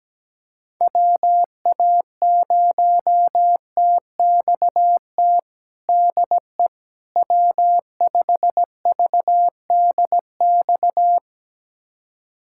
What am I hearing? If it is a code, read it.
Morse code, heard as a keyed tone, about 17 wpm, 700 Hz. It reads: WA0TXT DE W5VDX
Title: USA Callsign DE Callsign 17wpm